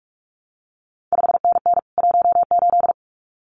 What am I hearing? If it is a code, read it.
Morse code, heard as a keyed tone, about 46 wpm, 700 Hz. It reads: 5NN 18
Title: CQ WW DX Contest Exchange Encore - 45wpm